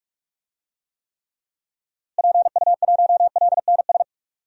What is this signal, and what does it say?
Morse code, heard as a keyed tone, about 46 wpm, 700 Hz. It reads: PU1LNS
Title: DXCC-100 Call Signs - 6 Characters - Repeated 3x - 45wpm